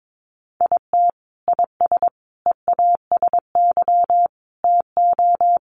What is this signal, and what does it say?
Morse code, heard as a keyed tone, about 22 wpm, 700 Hz. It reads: IT IS EASY TO
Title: Sentences from Top 1100 Words 22wpm